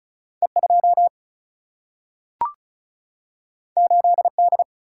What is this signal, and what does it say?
Morse code, heard as a keyed tone, about 34 wpm, 700 Hz. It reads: E2 8D
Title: Single-Letter Single-Number 35wpm